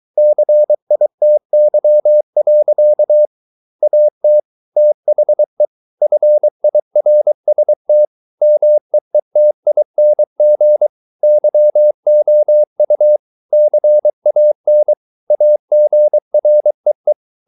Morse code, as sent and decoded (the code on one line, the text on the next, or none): -.-. .. - -.-- .-.-.- / .- - / - .... / ..-. .. .-. ... - / -- . - .. -. --. / -.-- --- ..- / -.-. .- -. / .- --. .-. .
CITY. AT THE FIRST MEETING YOU CAN AGREE